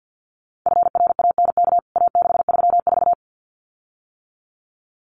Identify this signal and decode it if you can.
Morse code, heard as a keyed tone, about 48 wpm, 700 Hz. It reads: FRANK A634
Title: CWT Exchange 50wpm